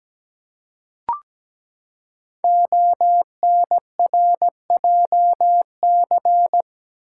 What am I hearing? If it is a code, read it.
Morse code, heard as a keyed tone, about 17 wpm, 700 Hz. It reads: ONRJC
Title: Code Groups - 5 Letters - 17wpm